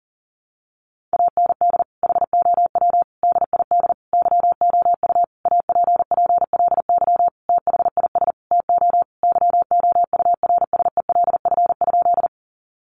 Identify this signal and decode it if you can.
Morse code, heard as a keyed tone, about 40 wpm, 700 Hz. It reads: AND HOW DID YOU APPLY THIS TO YOURSELF?
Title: Sentences from Top 1200 Words 40wpm